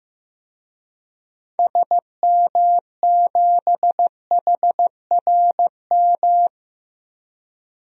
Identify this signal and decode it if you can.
Morse code, heard as a keyed tone, about 15 wpm, 700 Hz. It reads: SM7HRM
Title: DXCC-100 Call Signs - 6 Characters - 15wpm